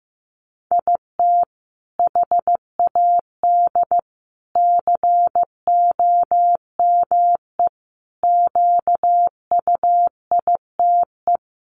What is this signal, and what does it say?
Morse code, heard as a keyed tone, about 14 wpm, 700 Hz. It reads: IT HAD COME QUITE
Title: Sentences from Top 1300 Words 15wpm